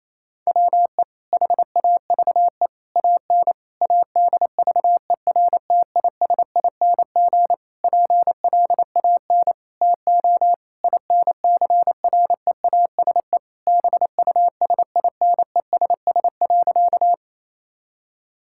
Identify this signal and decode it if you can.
Morse code, heard as a keyed tone, about 27 wpm, 700 Hz. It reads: WE HAVE AN ADVERTISING PLAN TO INCREASE BUSINESS.
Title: Sentences from Top 1000 Words - Review Part 6 - 28wpm